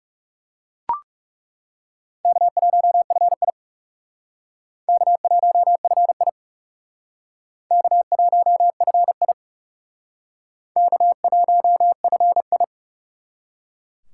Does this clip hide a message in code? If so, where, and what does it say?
Yes: Morse code from 2.2 to 12.6 s, heard as a keyed tone, about 38 wpm, 700 Hz. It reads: K1FI K1FI K1FI K1FI